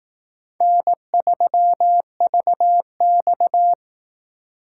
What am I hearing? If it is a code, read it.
Morse code, heard as a keyed tone, about 18 wpm, 700 Hz. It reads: N3VX
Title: Speed Racing - USA 1x2 Call Signs 15wpm